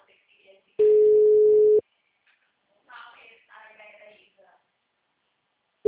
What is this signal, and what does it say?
Ringback tone (Europe, 425 Hz), with speech.